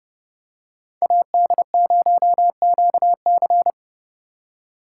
Morse code, heard as a keyed tone, about 30 wpm, 700 Hz. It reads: AD0QC